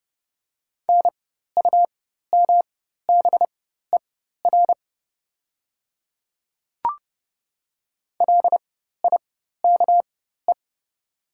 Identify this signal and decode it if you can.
Morse code, heard as a keyed tone, about 30 wpm, spaced as for 14 wpm, 700 Hz. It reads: NUMBER LIKE
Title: 4x Character Spacing - Top 200 Words - 30wpm